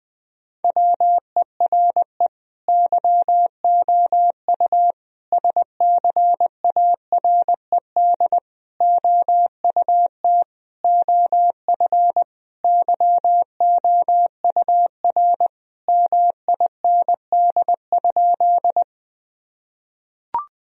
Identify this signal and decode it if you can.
Morse code, heard as a keyed tone, about 20 wpm, 700 Hz. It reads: WERE YOU SCARED OUT OF YOUR MIND?
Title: Sentences from Top 1000 Words - Review Part 4 - 20wpm